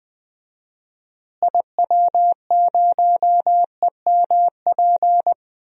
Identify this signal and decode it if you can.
Morse code, heard as a keyed tone, about 20 wpm, 700 Hz. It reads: IW0EMP
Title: DXCC-100 Call Signs - 6 Characters - Repeated 3x - 20wpm